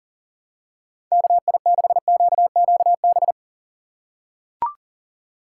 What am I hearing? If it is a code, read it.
Morse code, heard as a keyed tone, about 40 wpm, 700 Hz. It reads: KI6QQB
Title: USA Call Signs - 6 Characters - 3x Character Spacing - 40wpm